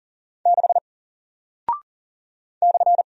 Morse code, heard as a keyed tone, about 40 wpm, 700 Hz. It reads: 6 /
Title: Single Character (Letter:Number:Punctuation:Prosign) Alt 40wpm